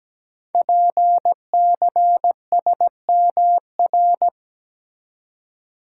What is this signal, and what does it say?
Morse code, heard as a keyed tone, about 17 wpm, 700 Hz. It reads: PCSMR
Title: Code Groups - 5 Letter:Numbers - 17wpm